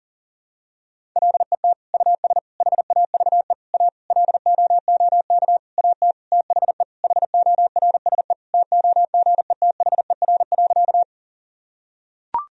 Morse code, heard as a keyed tone, about 40 wpm, 700 Hz. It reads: LET US HAVE A LOOK AT THE HORSE TOGETHER.
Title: Sentences from Top 1000 Words - Review Part 1 - 40wpm